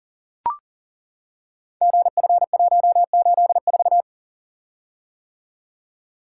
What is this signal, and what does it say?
Morse code, heard as a keyed tone, about 39 wpm, 700 Hz. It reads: GF184
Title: Code Groups - 5 Letter:Numbers - 40wpm